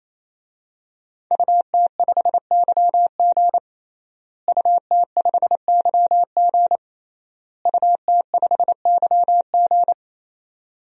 Morse code, heard as a keyed tone, about 28 wpm, 700 Hz. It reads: UT5YG UT5YG UT5YG
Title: DXCC-100 Call Signs - 5 Characters - Repeated 3x - 28wpm